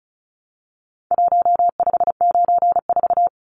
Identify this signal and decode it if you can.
Morse code, heard as a keyed tone, about 34 wpm, 700 Hz. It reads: 1594
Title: Numbers - Contest Style - 4-Digit 35wpm